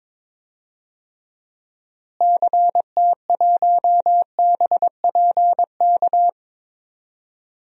Morse code, heard as a keyed tone, about 22 wpm, 700 Hz. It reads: CT1BPK
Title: DXCC-100 Call Signs - 6 Characters - Repeated 3x - 22wpm